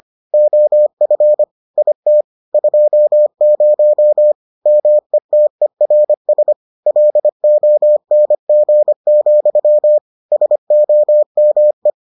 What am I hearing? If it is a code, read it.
Morse code, heard as a keyed tone, about 25 wpm, 600 Hz. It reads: OF IT 20 METERS LONG, SOME